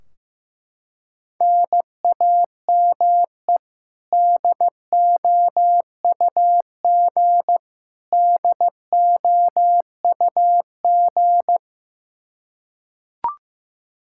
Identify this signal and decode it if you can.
Morse code, heard as a keyed tone, about 15 wpm, 700 Hz. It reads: NAME DOUG DOUG
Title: Practice QSO - Exchange 1 - Beginning - 15wpm